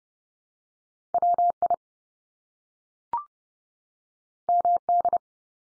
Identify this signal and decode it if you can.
Morse code, heard as a keyed tone, about 30 wpm, 700 Hz. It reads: WI MD
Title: US States 30WPM